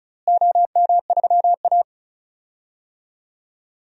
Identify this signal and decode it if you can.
Morse code, heard as a keyed tone, about 36 wpm, 700 Hz. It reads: OM3A
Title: DXCC-100 Call Signs - 4 Characters - Repeated 3x - 35wpm